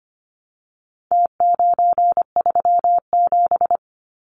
Morse code, heard as a keyed tone, about 25 wpm, 700 Hz. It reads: T937
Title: Numbers - Contest Style - 4-Digit 25wpm